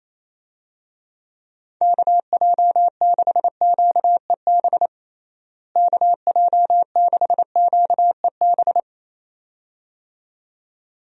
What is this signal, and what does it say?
Morse code, heard as a keyed tone, about 27 wpm, 700 Hz. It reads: KJ6QEB KJ6QEB
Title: USA Call Signs - 6 Characters - Repeated 2x - 28wpm